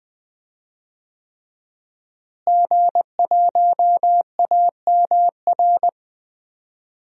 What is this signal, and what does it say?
Morse code, heard as a keyed tone, about 19 wpm, 700 Hz. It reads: G1AMR
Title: DXCC-100 Call Signs - 5 Characters - Repeated 3x - 20wpm